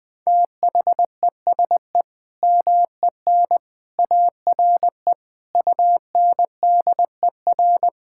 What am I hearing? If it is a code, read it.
Morse code, heard as a keyed tone, about 20 wpm, 700 Hz. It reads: THESE MEN ARE UNDER
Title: Sentences from Top 1000 Words - Review Part 6 - 20wpm